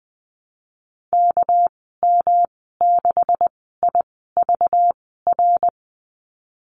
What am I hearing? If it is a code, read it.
Morse code, heard as a keyed tone, about 19 wpm, spaced as for 14 wpm, 700 Hz. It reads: KM6IVR
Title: 213 Lesson 34g - Number 8 - Call Signs 20wpm